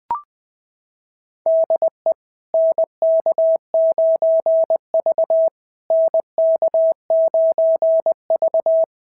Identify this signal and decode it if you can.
Morse code, heard as a keyed tone, about 18 wpm, 650 Hz. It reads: DE NK9V NK9V
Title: SOTA QSO Practice 20wpm